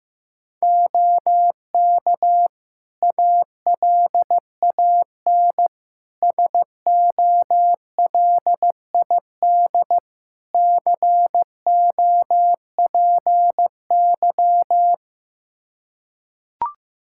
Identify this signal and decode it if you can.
Morse code, heard as a keyed tone, about 15 wpm, 700 Hz. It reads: OK ALAN SOLID COPY
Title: Practice QSO - Exchange 3 - More Info - 15wpm